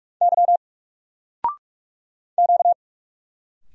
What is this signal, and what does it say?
Morse code, heard as a keyed tone, about 44 wpm, 700 Hz. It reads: Y Q
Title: Single Letter-Number 45wpm